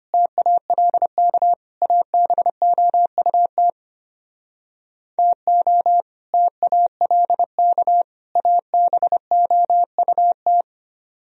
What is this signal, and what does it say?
Morse code, heard as a keyed tone, about 33 wpm, 700 Hz. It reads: TALK ABOUT TO TALK ABOUT